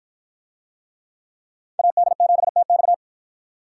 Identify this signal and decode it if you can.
Morse code, heard as a keyed tone, about 53 wpm, 700 Hz. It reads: AD7TX